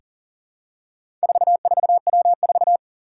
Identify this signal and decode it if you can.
Morse code, heard as a keyed tone, about 40 wpm, 700 Hz. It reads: 44W4
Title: Four Challenging Letter-Number Combinations 40wpm